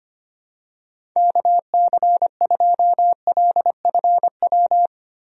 Morse code, heard as a keyed tone, about 25 wpm, 700 Hz. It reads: KC2LFW
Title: USA Call Signs - 6 Characters - Repeated 2x - 25wpm